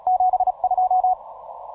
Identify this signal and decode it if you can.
Morse code, heard as a keyed tone, about 36 wpm, 740 Hz. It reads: 72
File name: simple-file-list / 9N7AA 12MCW